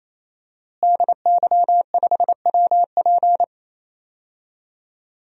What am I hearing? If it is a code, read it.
Morse code, heard as a keyed tone, about 28 wpm, 700 Hz. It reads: DY5WP
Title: DXCC-100 Call Signs - 5 Characters - Repeated 3x - 28wpm